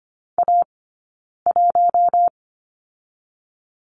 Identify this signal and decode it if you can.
Morse code, heard as a keyed tone, about 25 wpm, 700 Hz. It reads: A 1
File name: course / 25:10wpm / 092 Lesson 16d - Letter c - Two Characters 25wpm